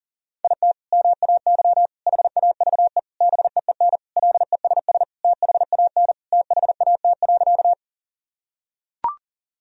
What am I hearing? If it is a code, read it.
Morse code, heard as a keyed tone, about 40 wpm, 700 Hz. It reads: IT MAY HAVE BEEN LESS THAN THAT.
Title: Sentences from Top 1000 Words - Review Part 2 - 40wpm